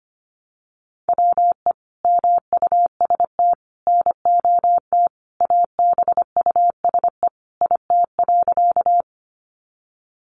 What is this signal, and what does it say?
Morse code, heard as a keyed tone, about 25 wpm, 700 Hz. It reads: WE MUST NOT ABUSE IT.
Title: Sentences from Top 1300 Words 25wpm